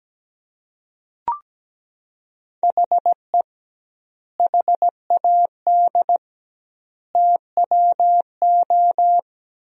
Morse code, heard as a keyed tone, about 17 wpm, 700 Hz. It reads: HE HAD TWO
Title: 2x Word-Spacing - Sentences from Top 300 Words - 17wpm